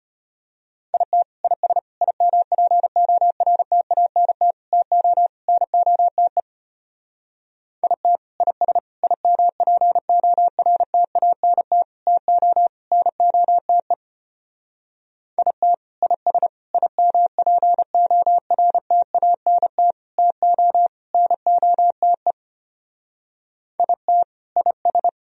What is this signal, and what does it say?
Morse code, heard as a keyed tone, about 30 wpm, 700 Hz. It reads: IT IS IMPORTANT TO NOTE IT IS IMPORTANT TO NOTE IT IS IMPORTANT TO NOTE IT IS
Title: Speed-Racing - Sets of 5 Words 25wpm